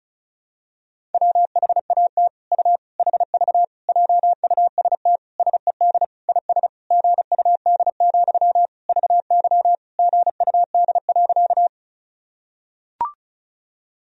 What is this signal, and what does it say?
Morse code, heard as a keyed tone, about 34 wpm, 700 Hz. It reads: WHAT U HV JUST SED IS GUD, VY GUD.
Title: Sentences from Top 200 Words and Abbreviations 35wpm